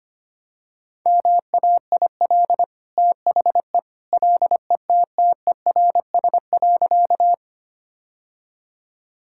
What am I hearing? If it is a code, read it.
Morse code, heard as a keyed tone, about 25 wpm, 700 Hz. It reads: MAIL THE LETTERS.
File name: Easy Sentences with Hints - 25wpm